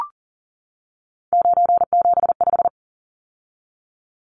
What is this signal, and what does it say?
Morse code, heard as a keyed tone, about 40 wpm, 700 Hz. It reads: 975
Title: Numbers - 3-Digit 40wpm